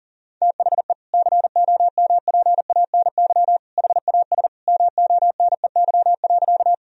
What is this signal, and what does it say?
Morse code, heard as a keyed tone, about 40 wpm, 700 Hz. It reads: THE COMPANY HAS MONEY.
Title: Sentences from Top 1000 Words - Review Part 3 - 40wpm